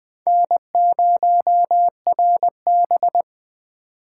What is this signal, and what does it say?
Morse code, heard as a keyed tone, about 20 wpm, 700 Hz. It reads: N0RB
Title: Four Letter-Number Combinations - 20wpm